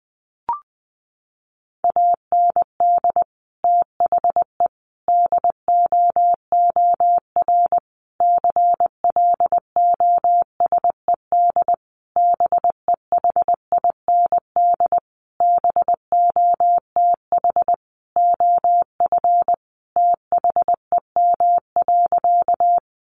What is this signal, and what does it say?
Morse code, heard as a keyed tone, about 20 wpm, 700 Hz. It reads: AND THE DOOR CLOSED BEHIND BOTH OF THEM.
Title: Sentences from Top 1400 Words 20wpm